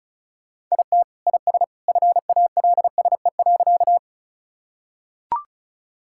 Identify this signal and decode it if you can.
Morse code, heard as a keyed tone, about 34 wpm, 700 Hz. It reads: IT IS FALSE.
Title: Sentences from Top 1600 Words 35wpm